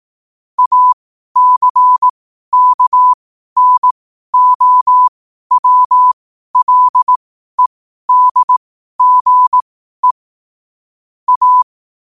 Morse code, heard as a keyed tone, about 18 wpm, spaced as for 12 wpm, 1000 Hz. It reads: ACKNOWLEDGE A
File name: qotdfiles / 12wpm